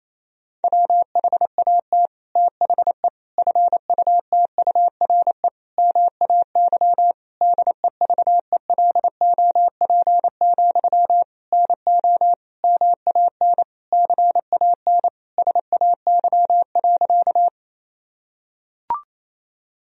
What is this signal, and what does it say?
Morse code, heard as a keyed tone, about 28 wpm, 700 Hz. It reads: WHAT THE FUTURE MAY DEVELOP, NO MAN CAN SAY.
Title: Sentences from Top 1000 Words - Review Part 2 - 28wpm